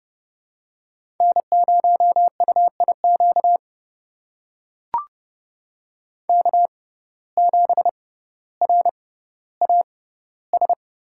Morse code, heard as a keyed tone, about 30 wpm, spaced as for 15 wpm, 700 Hz. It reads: N0UIQ K7RAS
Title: USA Call Signs - 5 Characters - 6x Character Spacing - 30wpm